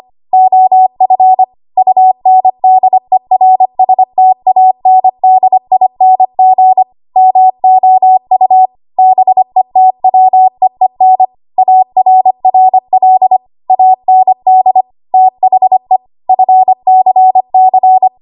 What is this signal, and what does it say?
Morse code, heard as a keyed tone, about 25 wpm, 750 Hz. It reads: OF UNDERSTANDING MOU BETWEEN ARRL AND THE FCC
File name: Archive / 25 WPM / 25WPM